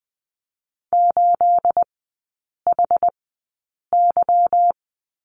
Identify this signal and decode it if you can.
Morse code, heard as a keyed tone, about 20 wpm, spaced as for 8 wpm, 690 Hz. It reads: 8HY
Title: 212 Lesson 34e - Number 8 - Three Characters 20wpm